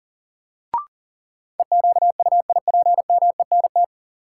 Morse code, heard as a keyed tone, about 40 wpm, 700 Hz. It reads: EQUIPMENT
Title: Sentences from Top 700 Words - New Word Drill - 40wpm